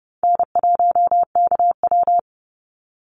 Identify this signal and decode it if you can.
Morse code, heard as a keyed tone, about 30 wpm, 700 Hz. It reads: N1KW
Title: Easy Call Signs Repeated 2x - 30wpm